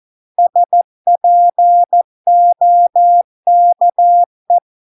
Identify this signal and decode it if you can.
Morse code, heard as a keyed tone, about 14 wpm, 700 Hz. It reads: SPOKE